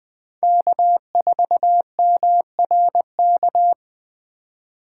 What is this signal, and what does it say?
Morse code, heard as a keyed tone, about 20 wpm, 700 Hz. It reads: K4MRK
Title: USA Call Signs - 5 Characters - 4x Character Spacing - 20wpm